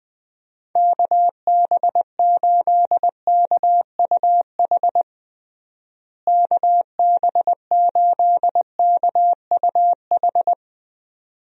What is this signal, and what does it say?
Morse code, heard as a keyed tone, about 20 wpm, 700 Hz. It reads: KB8KUH KB8KUH
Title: USA Call Signs - 6 Characters - Repeated 2x - 20wpm